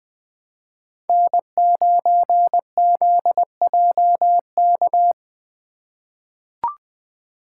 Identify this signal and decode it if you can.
Morse code, heard as a keyed tone, about 20 wpm, 700 Hz. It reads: N9ZJK
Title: USA Call Signs - 5 Characters - 3x Character Spacing - 20wpm